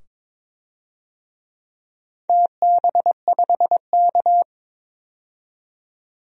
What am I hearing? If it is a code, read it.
Morse code, heard as a keyed tone, about 22 wpm, 700 Hz. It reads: TB5K
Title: DXCC-100 Call Signs - 4 Characters - 22wpm